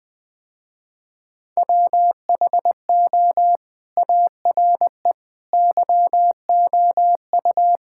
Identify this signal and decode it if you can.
Morse code, heard as a keyed tone, about 20 wpm, 700 Hz. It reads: WHO ARE YOU
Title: Sentences with 2-Word Phrases - Spoken Beginning - 20wpm